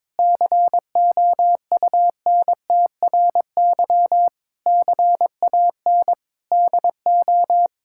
Morse code, heard as a keyed tone, about 22 wpm, 700 Hz. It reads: COUNTRY CAN DO
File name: Sets of 9 Words - 22wpm